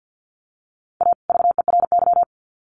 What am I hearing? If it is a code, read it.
Morse code, heard as a keyed tone, about 50 wpm, 700 Hz. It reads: A VERY
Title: Sets of 2 Words 50wpm